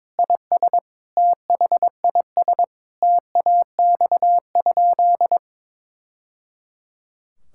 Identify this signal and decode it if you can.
Morse code, heard as a keyed tone, about 22 wpm, 700 Hz. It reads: IS THIS TAX?
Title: Sentences from Top 1000 Words - Review Part 5 - 22wpm